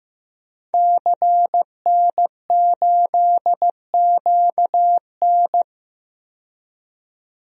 Morse code, heard as a keyed tone, about 15 wpm, 700 Hz. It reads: CN8QN
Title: DXCC-100 Call Signs - 5 Characters - 15wpm